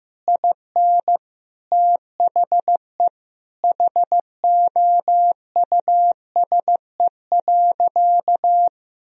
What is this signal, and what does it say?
Morse code, heard as a keyed tone, about 15 wpm, 700 Hz. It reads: IN THE HOUSE.
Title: Sentences from Top 1000 Words - Review Part 4 - 15wpm